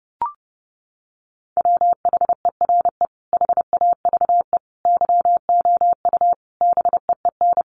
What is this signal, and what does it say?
Morse code, heard as a keyed tone, about 30 wpm, 700 Hz. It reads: WHERE HAVE YOU BEEN
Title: Sentences from Top 300 Words 30wpm